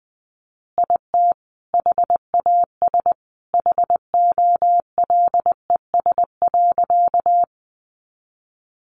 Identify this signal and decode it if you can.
Morse code, heard as a keyed tone, about 20 wpm, 700 Hz. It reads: IT HAS HOLES.
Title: Additional Easy Phrases 20wpm